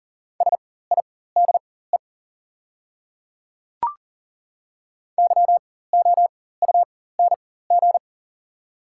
Morse code, heard as a keyed tone, about 40 wpm, spaced as for 19 wpm, 700 Hz. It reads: SIDE YOUNG
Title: 4x Character Spacing - Top 500 Words - 40wpm